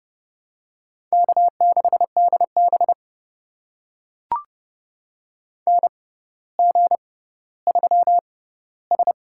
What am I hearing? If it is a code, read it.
Morse code, heard as a keyed tone, about 28 wpm, spaced as for 15 wpm, 700 Hz. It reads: K6DB NG3S
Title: USA Call Signs - 4 Characters - 6x Character Spacing - 30wpm